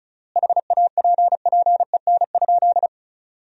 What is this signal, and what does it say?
Morse code, heard as a keyed tone, about 34 wpm, 700 Hz. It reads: HAPPEN?
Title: 3x Word-Spacing - Sentences from Top 400 Words - 35wpm